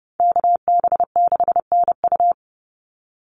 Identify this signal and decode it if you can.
Morse code, heard as a keyed tone, about 30 wpm, 700 Hz. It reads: KB6NU
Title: Call Signs Repeated 3x - 30wpm